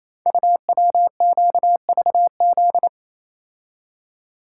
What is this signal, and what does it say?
Morse code, heard as a keyed tone, about 27 wpm, 700 Hz. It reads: UWQVZ
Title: Code Groups - 5 Letters - 28wpm